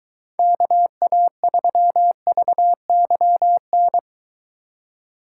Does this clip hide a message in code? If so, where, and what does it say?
Yes: Morse code from 0.4 to 4.0 s, heard as a keyed tone, about 23 wpm, 700 Hz. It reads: KA3VYN